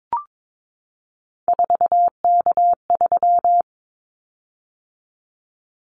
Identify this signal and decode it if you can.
Morse code, heard as a keyed tone, about 22 wpm, 700 Hz. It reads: 4K3
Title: Callsign Prefixes for Top 100 DXCC - 22wpm